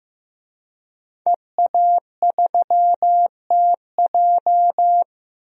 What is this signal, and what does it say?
Morse code, heard as a keyed tone, about 15 wpm, 700 Hz. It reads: EA3TJ
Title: DXCC-100 Call Signs - 5 Characters - Repeated 3x - 15wpm